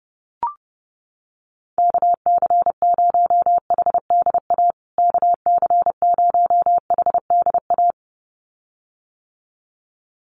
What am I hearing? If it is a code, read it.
Morse code, heard as a keyed tone, about 30 wpm, 700 Hz. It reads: KC0HDA KC0HDA
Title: USA Call Signs Repeated 2x 30wpm